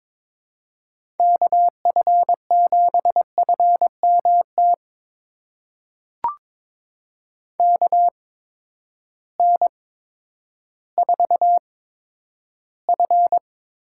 Morse code, heard as a keyed tone, about 22 wpm, spaced as for 11 wpm, 700 Hz. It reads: KF7FMT KN4F
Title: USA Call Signs - 8x Character Spacing - 22wpm